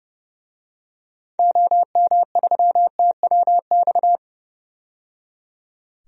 Morse code, heard as a keyed tone, about 30 wpm, 700 Hz. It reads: OM3TWX
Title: DXCC-100 Call Signs - 6 Characters - 30wpm